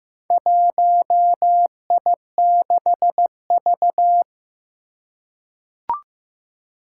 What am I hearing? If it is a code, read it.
Morse code, heard as a keyed tone, about 15 wpm, 700 Hz. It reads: 1I6V
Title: Four Challenging Letter-Number Combinations 15wpm